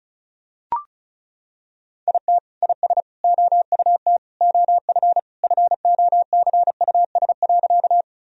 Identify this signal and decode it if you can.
Morse code, heard as a keyed tone, about 34 wpm, 700 Hz. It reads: IT IS OUT OF FOCUS.
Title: Sentences from Top 1000 Words - Review Part 4 - 35wpm